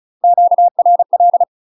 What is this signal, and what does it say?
Morse code, heard as a keyed tone, about 34 wpm, 700 Hz. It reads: QRL
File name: UK MASTER QSOs TRAINING/35WPM